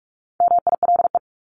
Morse code, heard as a keyed tone, about 44 wpm, 700 Hz. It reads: MILE
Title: Top 500 Words Repeated 45wpm